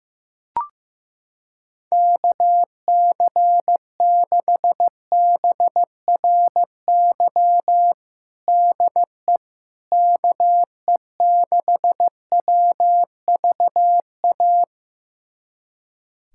Morse code, heard as a keyed tone, about 15 wpm, 700 Hz. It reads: KC6BRY DE KE6WVA
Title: USA Callsign DE Callsign - 6 Characters - 15wpm